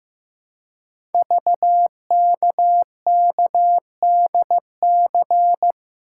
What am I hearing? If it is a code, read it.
Morse code, heard as a keyed tone, about 15 wpm, 700 Hz. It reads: VKKDC
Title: Code Groups - 5 Letters - 15wpm